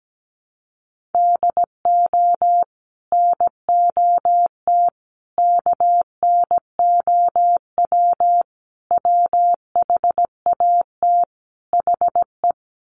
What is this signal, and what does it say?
Morse code, heard as a keyed tone, about 17 wpm, 700 Hz. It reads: DO NOT KNOW WHAT HE
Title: Sets of 5 Words Encore2 17wpm